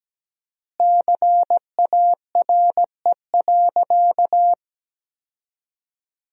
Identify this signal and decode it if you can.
Morse code, heard as a keyed tone, about 17 wpm, 700 Hz. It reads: CARE.
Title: 4x Word-Spacing - Sentences from Top 300 Words - 17wpm